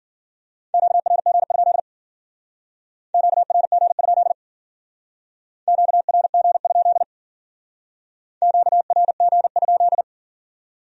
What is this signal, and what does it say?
Morse code, heard as a keyed tone, about 50 wpm, 700 Hz. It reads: QRG? QRG? QRG? QRG?